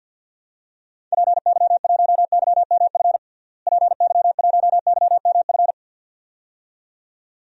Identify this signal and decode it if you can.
Morse code, heard as a keyed tone, about 46 wpm, 700 Hz. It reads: PY1YMF PY1YMF
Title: DXCC-100 Call Signs - 6 Characters - Repeated 2x - 50wpm